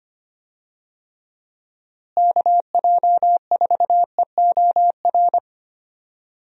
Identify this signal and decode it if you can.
Morse code, heard as a keyed tone, about 25 wpm, 700 Hz. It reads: KJ4EOR
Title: USA Call Signs - 6 Characters - Repeated 2x - 25wpm